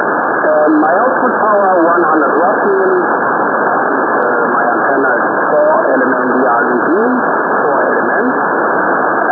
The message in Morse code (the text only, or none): none